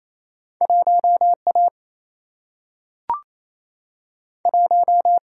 Morse code, heard as a keyed tone, about 28 wpm, 700 Hz. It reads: 1 A 1
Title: Single-Letter Single-Number 28wpm